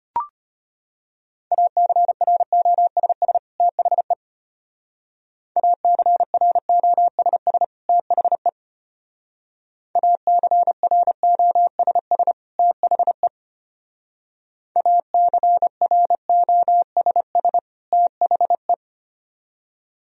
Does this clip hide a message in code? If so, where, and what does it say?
Yes: Morse code from 1.5 to 18.7 s, heard as a keyed tone, about 32 wpm, 700 Hz. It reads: ACROSS THE ACROSS THE ACROSS THE ACROSS THE